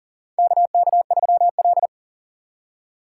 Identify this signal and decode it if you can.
Morse code, heard as a keyed tone, about 40 wpm, 700 Hz. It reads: KK3L
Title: USA Call Signs - 4 Characters - 3x Character Spacing - 40wpm